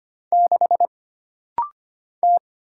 Morse code, heard as a keyed tone, about 25 wpm, 700 Hz. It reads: 6 T
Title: Single Letter-Number - Rapid-Fire 25wpm